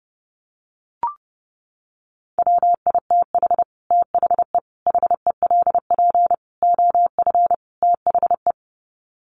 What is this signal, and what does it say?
Morse code, heard as a keyed tone, about 30 wpm, 700 Hz. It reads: WITH THE HELP OF THE
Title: Sets of 5 Words Encore 30wpm